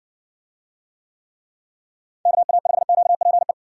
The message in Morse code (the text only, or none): KA5QPE